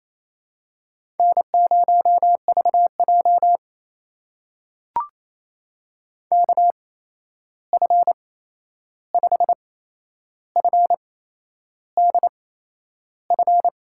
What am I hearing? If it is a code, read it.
Morse code, heard as a keyed tone, about 28 wpm, spaced as for 11 wpm, 700 Hz. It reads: N0VJ KF5FDF
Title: USA Call Signs - 8x Character Spacing - 28wpm